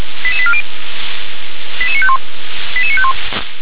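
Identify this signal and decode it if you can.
Five-tone selective call (ZVEI), heard as quick runs of five tones, tones of about 70 ms. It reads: 88840 88951 88951
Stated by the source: ZVEI five-tone selective call 88955, received by SDR.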